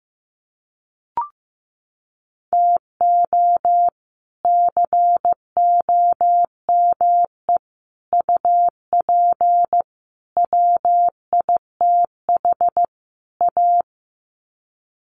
Morse code, heard as a keyed tone, about 15 wpm, 700 Hz. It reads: TO COME UP WITH A
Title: Sets of 5 Words 15wpm